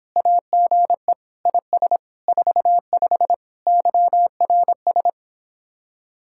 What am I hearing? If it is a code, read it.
Morse code, heard as a keyed tone, about 26 wpm, 700 Hz. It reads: AGE IS 45 YRS